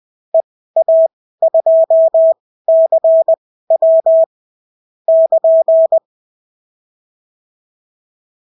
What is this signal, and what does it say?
Morse code, heard as a keyed tone, about 20 wpm, spaced as for 14 wpm, 650 Hz. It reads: EA2CW <KN>
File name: CW QSO libroCW4 20-10ppm